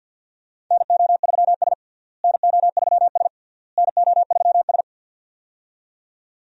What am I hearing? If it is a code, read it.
Morse code, heard as a keyed tone, about 52 wpm, 700 Hz. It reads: NO3S NO3S NO3S
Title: USA Call Signs - 4 Characters - Repeated 3x - 50wpm